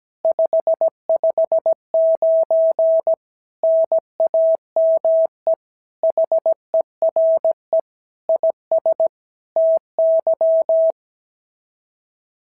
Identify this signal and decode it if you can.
Morse code, heard as a keyed tone, about 16 wpm, 650 Hz. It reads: 559 NAME HERE IS TY
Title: SOTA QSO Practice 17wpm